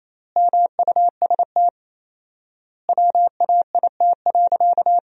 Morse code, heard as a keyed tone, about 28 wpm, 700 Hz. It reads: MUST WAIT.
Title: 4x Word-Spacing - Sentences from Top 500 Words - 28wpm